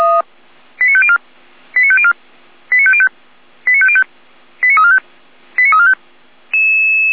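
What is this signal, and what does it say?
Five-tone selective call (ZVEI), heard as quick runs of five tones, tones of about 70 ms. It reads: 89504 89504 89505 89505 89345 89345
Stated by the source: Received on an SDR as a ZVEI five-tone call, code 89230.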